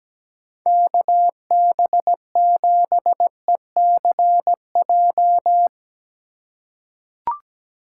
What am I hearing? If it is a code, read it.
Morse code, heard as a keyed tone, about 17 wpm, 700 Hz. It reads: KB7ECJ
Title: USA Call Signs - 4x Character Spacing - 17wpm